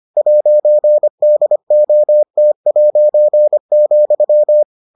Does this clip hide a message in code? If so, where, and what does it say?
Yes: Morse code from 0.2 to 4.6 s, heard as a keyed tone, about 25 wpm, 600 Hz. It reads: 'DOT',